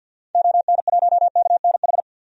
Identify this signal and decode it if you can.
Morse code, heard as a keyed tone, about 46 wpm, 700 Hz. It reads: ON1KNH
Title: DXCC-100 Call Signs - 6 Characters - 50wpm